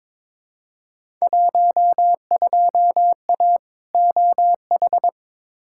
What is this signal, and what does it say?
Morse code, heard as a keyed tone, about 22 wpm, 700 Hz. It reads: 12A OH
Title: Field Day Exchanges Encore - 22wpm